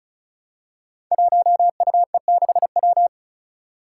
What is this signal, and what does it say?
Morse code, heard as a keyed tone, about 36 wpm, 700 Hz. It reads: 1UE6W
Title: Five Challenging Letter-Number Combinations 35wpm